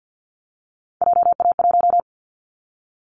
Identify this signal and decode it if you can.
Morse code, heard as a keyed tone, about 49 wpm, 700 Hz. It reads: JA1
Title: Callsign Prefixes for Top 25 DXCC - 50wpm